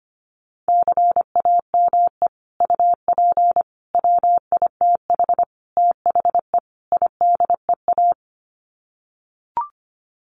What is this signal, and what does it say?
Morse code, heard as a keyed tone, about 25 wpm, 700 Hz. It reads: CAME UP WITH THE IDEA
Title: Sets of 5 Words Encore2 25wpm